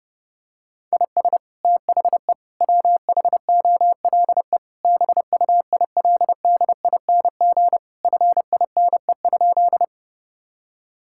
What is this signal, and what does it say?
Morse code, heard as a keyed tone, about 32 wpm, 700 Hz. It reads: IS THE WHOLE BUILDING FINE?
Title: Sentences from Top 1000 Words - Review Part 5 - 30wpm